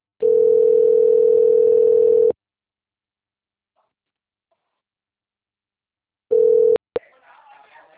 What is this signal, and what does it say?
Ringback tone (North America), 440 + 480 Hz.